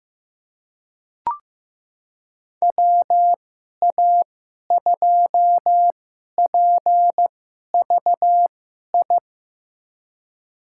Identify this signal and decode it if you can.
Morse code, heard as a keyed tone, about 15 wpm, spaced as for 11 wpm, 700 Hz. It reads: WA2PVI
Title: USA Call Signs - 6 Characters - 2x Character Spacing - 15wpm